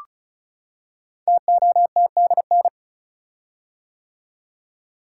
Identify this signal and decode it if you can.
Morse code, heard as a keyed tone, about 36 wpm, 700 Hz. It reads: TOTDN
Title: Code Groups - 5 Letters - 35wpm